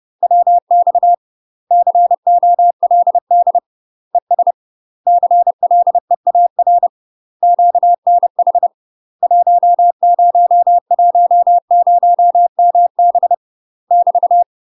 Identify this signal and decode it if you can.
Morse code, heard as a keyed tone, about 30 wpm, 700 Hz. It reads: WX COLD ES CLEAR QNH 1010MB <BT>
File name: UK MASTER QSOs TRAINING / 30WPM / UK MASTER QSOs 0000